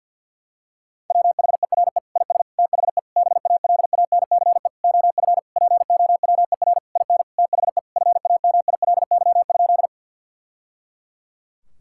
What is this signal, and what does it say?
Morse code, heard as a keyed tone, about 49 wpm, 700 Hz. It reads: WHERE IS THE BALANCE OF POWER IN THE FAMILY?
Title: Sentences from Top 1000 Words - Review Part 6 - 50wpm